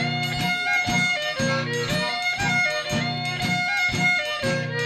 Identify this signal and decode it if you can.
Emergency Alert System (SAME) tones, heard as harsh buzzing bursts. It is heard from 0.5 to 4.3 s.